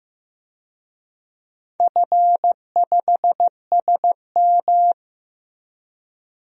Morse code, heard as a keyed tone, about 15 wpm, 700 Hz. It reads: F5SM
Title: DXCC-100 Call Signs - 4 Characters - Repeated 2x - 15wpm